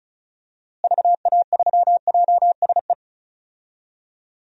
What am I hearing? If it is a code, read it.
Morse code, heard as a keyed tone, about 36 wpm, 700 Hz. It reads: VA3JSE
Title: DXCC-100 Call Signs - 6 Characters - Repeated 3x - 35wpm